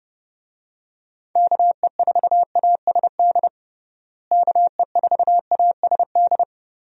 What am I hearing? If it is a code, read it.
Morse code, heard as a keyed tone, about 32 wpm, 700 Hz. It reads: KE4ASD KE4ASD
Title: USA Call Signs - 6 Characters - Repeated 2x - 30wpm